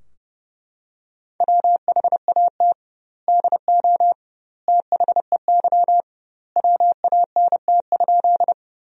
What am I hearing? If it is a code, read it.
Morse code, heard as a keyed tone, about 30 wpm, 700 Hz. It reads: WHAT DO THEY WANT?
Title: 2x Word-Spacing - Sentences from Top 200 Words - 30wpm